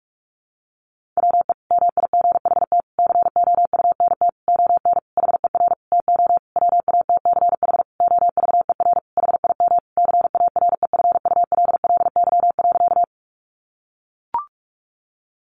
Morse code, heard as a keyed tone, about 44 wpm, 700 Hz. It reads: WE MIGHT COUNT ON HER TO WATCH OVER HIM CAREFULLY.
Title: Sentences from Top 1500 Words 45wpm